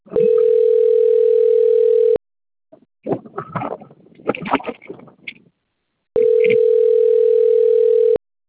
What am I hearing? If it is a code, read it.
Ringback tone (North America), 440 + 480 Hz.